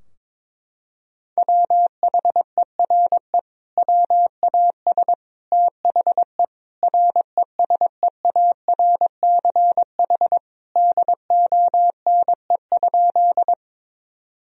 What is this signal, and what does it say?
Morse code, heard as a keyed tone, about 22 wpm, 700 Hz. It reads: WHERE WAS THE RESEARCH DONE?
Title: Sentences from Top 1000 Words - Review Part 4 - 22wpm